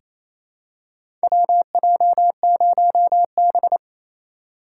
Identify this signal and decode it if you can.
Morse code, heard as a keyed tone, about 28 wpm, 700 Hz. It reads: WJ0B
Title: USA Call Signs - 4 Characters - 2x Character Spacing - 28wpm